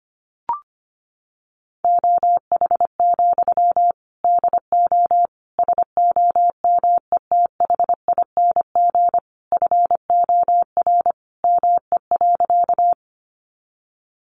Morse code, heard as a keyed tone, about 25 wpm, 700 Hz. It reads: OH, DO SOMETHING FOR ME.
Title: Sentences from Top 600 Words 25wpm